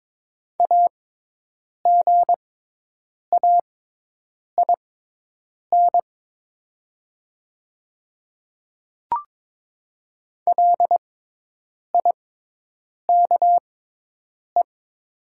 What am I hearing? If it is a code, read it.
Morse code, heard as a keyed tone, about 22 wpm, spaced as for 8 wpm, 700 Hz. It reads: AGAIN LIKE
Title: 6x Character Spacing - Top 200 Words - 22wpm